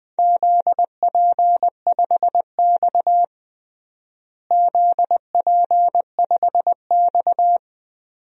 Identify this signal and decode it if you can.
Morse code, heard as a keyed tone, about 20 wpm, 700 Hz. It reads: ZP5X ZP5X
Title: DXCC-100 Call Signs - 4 Characters - Repeated 2x - 20wpm